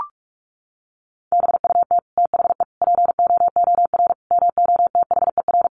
Morse code, heard as a keyed tone, about 44 wpm, 700 Hz. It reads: BUT THE POOR MOTHER
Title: Sentences from Top 1000 Words 45wpm